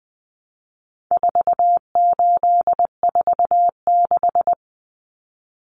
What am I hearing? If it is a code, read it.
Morse code, heard as a keyed tone, about 20 wpm, 700 Hz. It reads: 4846
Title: Common Cut Numbers - 4-Digit 20wpm